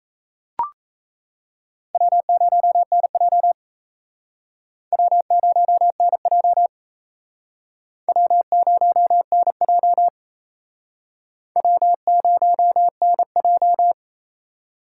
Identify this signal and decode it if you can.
Morse code, heard as a keyed tone, about 34 wpm, 700 Hz. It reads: W0NJ W0NJ W0NJ W0NJ
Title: Speed Racing - USA 1x2 Call Signs 28wpm